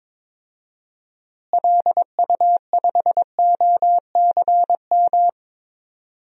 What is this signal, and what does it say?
Morse code, heard as a keyed tone, about 21 wpm, 700 Hz. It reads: LU5OCM
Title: DXCC-100 Call Signs - 6 Characters - Repeated 2x - 22wpm